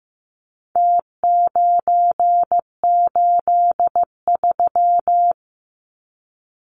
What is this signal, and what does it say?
Morse code, heard as a keyed tone, about 15 wpm, 700 Hz. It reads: T983
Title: Numbers - Contest Style - 4-Digit 15wpm